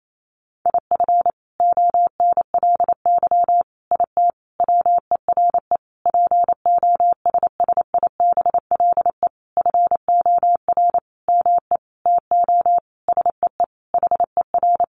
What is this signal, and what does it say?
Morse code, heard as a keyed tone, about 28 wpm, 700 Hz. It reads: IF ONLY IT WERE POSSIBLE FOR ME TO SEE HER
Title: Sentences from Top 500 Words 28wpm